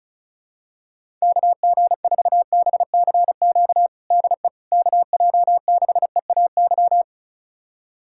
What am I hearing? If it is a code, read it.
Morse code, heard as a keyed tone, about 34 wpm, 700 Hz. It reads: KG4BCQ DE KJ6EAY
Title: USA Callsign DE Callsign - 6 Characters - 35wpm